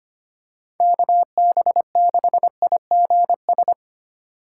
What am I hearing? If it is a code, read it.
Morse code, heard as a keyed tone, about 25 wpm, 700 Hz. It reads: KB6IGS
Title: USA Call Signs - 6 Characters - 4x Character Spacing - 25wpm